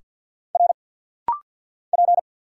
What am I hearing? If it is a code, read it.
Morse code, heard as a keyed tone, about 50 wpm, 700 Hz. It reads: R P
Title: Single Letters - Rapid-Fire 50wpm